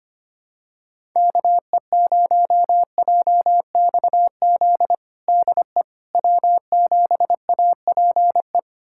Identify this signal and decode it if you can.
Morse code, heard as a keyed tone, about 25 wpm, 700 Hz. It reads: KE0JXZ DE W7APE
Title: USA Callsign DE Callsign 25wpm